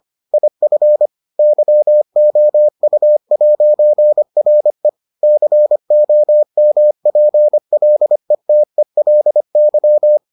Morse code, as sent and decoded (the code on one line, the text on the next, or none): .. ..-. / -.-- --- ..- .----. .-. / -.-. --- -- .--. .-.. - . .-.. -.--
IF YOU'RE COMPLETELY